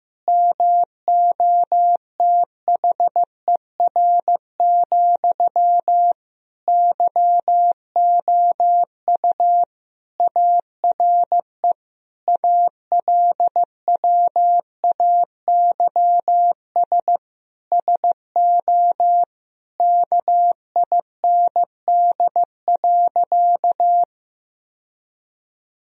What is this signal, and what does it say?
Morse code, heard as a keyed tone, about 15 wpm, 700 Hz. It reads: MOTHER, YOU ARE ALWAYS SO KIND.
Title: Sentences from Top 1000 Words - Review Part 3 - 15wpm